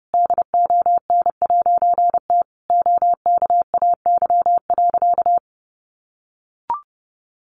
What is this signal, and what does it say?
Morse code, heard as a keyed tone, about 30 wpm, 700 Hz. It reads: DON'T OKAY.
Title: A Sound Of Thunder 30wpm